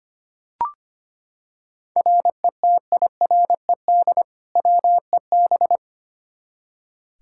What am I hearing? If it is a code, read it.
Morse code, heard as a keyed tone, about 25 wpm, 700 Hz. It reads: RETIRED WEB
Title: Practice QSO - Exchange 3 - More Info - 25wpm